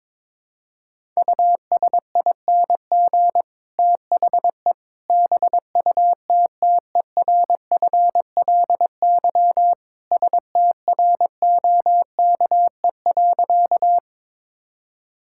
Morse code, heard as a keyed tone, about 21 wpm, 700 Hz. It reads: USING THE BUTTERFLY STROKE.
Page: morse-code-ninja-practice-22wpm